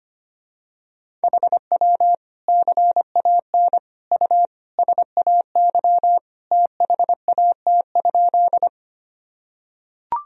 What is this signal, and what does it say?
Morse code, heard as a keyed tone, about 25 wpm, 700 Hz. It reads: HW CAN U SAY THAT?
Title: Sentences from Top 200 Words and Abbreviations 25wpm